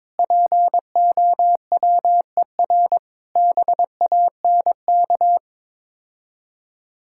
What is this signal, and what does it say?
Morse code, heard as a keyed tone, about 22 wpm, 690 Hz. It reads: POWER BANK
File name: Difficult Words with Hints - 22wpm